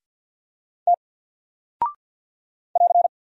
Morse code, heard as a keyed tone, about 50 wpm, 700 Hz. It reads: T <AR>
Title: Single Character (Letter:Number:Punctuation:Prosign) 50wpm